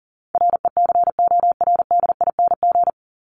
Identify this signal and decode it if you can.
Morse code, heard as a keyed tone, about 40 wpm, 700 Hz. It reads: RECORDING
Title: Top 1501-1750 Words 40wpm